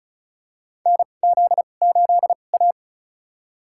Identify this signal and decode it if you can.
Morse code, heard as a keyed tone, about 34 wpm, spaced as for 25 wpm, 700 Hz. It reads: NZ8A
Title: USA Call Signs - 4 Characters - 2x Character Spacing - 35wpm